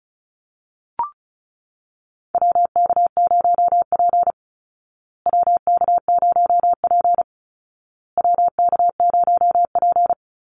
Morse code, heard as a keyed tone, about 34 wpm, 700 Hz. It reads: WK0P WK0P WK0P
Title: Call Signs Repeated 2x - 35wpm